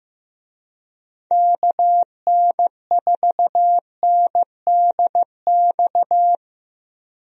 Morse code, heard as a keyed tone, about 15 wpm, 700 Hz. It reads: KN4NDX
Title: USA Call Signs - 6 Characters - 15wpm